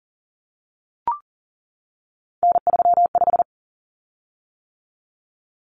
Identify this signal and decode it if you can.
Morse code, heard as a keyed tone, about 40 wpm, 700 Hz. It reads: N35
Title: Common Cut Numbers - 3-Digit 40wpm